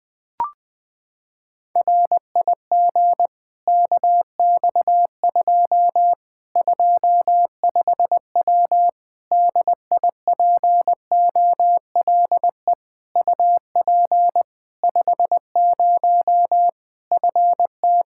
Morse code, heard as a keyed tone, about 20 wpm, 700 Hz. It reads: RIG KX2 25W DIPOLE UP 50 FT
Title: K1USN SST QSO Practice 20wpm